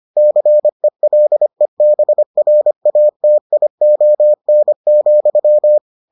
Morse code, heard as a keyed tone, about 25 wpm, 600 Hz. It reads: CELEBRATION,